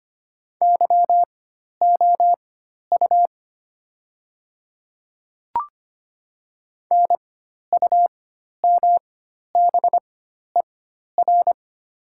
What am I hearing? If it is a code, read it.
Morse code, heard as a keyed tone, about 25 wpm, spaced as for 12 wpm, 690 Hz. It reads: YOU NUMBER
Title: 4x Character Spacing - Top 100 Words - 25wpm